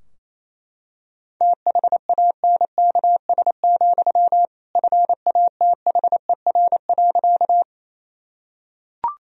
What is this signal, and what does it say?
Morse code, heard as a keyed tone, about 28 wpm, 700 Hz. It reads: THANKS, FATHER.
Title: Sentences from Top 1000 Words - Review Part 3 - 28wpm